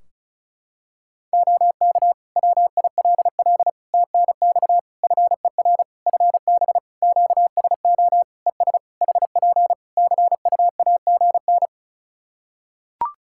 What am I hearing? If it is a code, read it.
Morse code, heard as a keyed tone, about 36 wpm, 700 Hz. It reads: OK WILL TNX FER FB QSO ES HP CUAGN
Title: Practice QSO - Exchange 4 - Ending - 35wpm